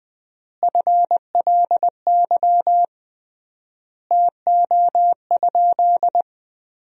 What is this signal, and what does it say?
Morse code, heard as a keyed tone, about 20 wpm, 700 Hz. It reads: FLY TO?
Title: 3x Word-Spacing - Sentences from Top 500 Words - 20wpm